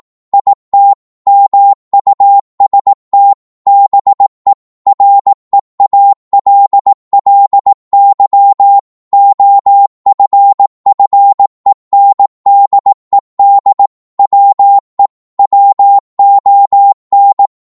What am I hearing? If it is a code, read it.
Morse code, heard as a keyed tone, about 18 wpm, 800 Hz. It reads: IT MUST BE REALLY OFFENDED WE WON
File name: Morse / psg-18wpm